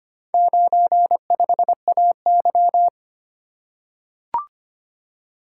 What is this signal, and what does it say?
Morse code, heard as a keyed tone, about 25 wpm, 700 Hz. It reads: 95AY
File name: practice / 25wpm / Four Letter-Number Combinations - 25wpm